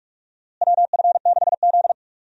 Morse code, heard as a keyed tone, about 42 wpm, 700 Hz. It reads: WF6Z
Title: USA Call Signs - 4 Characters - Repeated 3x - 45wpm